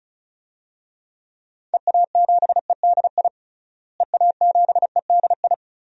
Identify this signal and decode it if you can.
Morse code, heard as a keyed tone, about 34 wpm, 700 Hz. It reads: EA7EDI EA7EDI
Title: DXCC-100 Call Signs - 6 Characters - Repeated 2x - 35wpm